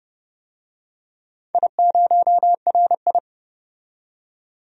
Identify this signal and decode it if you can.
Morse code, heard as a keyed tone, about 30 wpm, 700 Hz. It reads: I0RI
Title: DXCC-100 Call Signs - 4 Characters - 30wpm